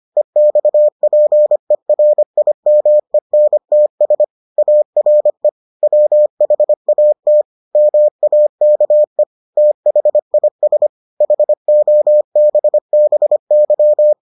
Morse code, heard as a keyed tone, about 25 wpm, 600 Hz. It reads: EXPERIMENTS ARE WHAT MAKE THIS HOBBY